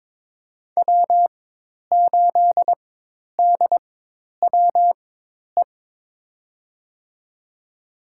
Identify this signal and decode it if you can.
Morse code, heard as a keyed tone, about 21 wpm, spaced as for 10 wpm, 700 Hz. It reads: W8DWE
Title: USA Call Signs - 4x Character Spacing - 22wpm